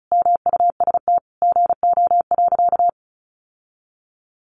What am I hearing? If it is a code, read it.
Morse code, heard as a keyed tone, about 34 wpm, 700 Hz. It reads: MUST GO.
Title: Sentences from Top 1200 Words 35wpm